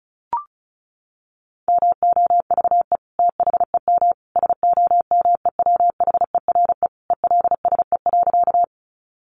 Morse code, heard as a keyed tone, about 34 wpm, 700 Hz. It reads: MOVE THEM SOMEWHERE ELSE.
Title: Sentences from Top 1400 Words 35wpm